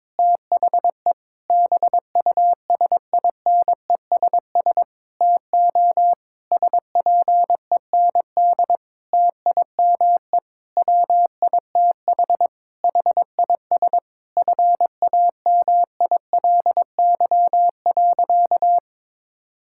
Morse code, heard as a keyed tone, about 22 wpm, 700 Hz. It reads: THE BUSINESS TO SPEND TIME WITH HIS FAMILY.
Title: Sentences from Top 1000 Words - Review Part 6 - 22wpm